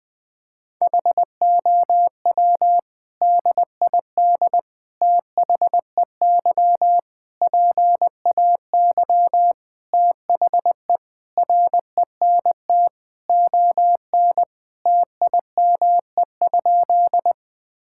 Morse code, heard as a keyed tone, about 20 wpm, 700 Hz. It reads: HOW DID THEY PAY THE RENT ON TIME?
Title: Sentences from Top 1000 Words - Review Part 5 - 20wpm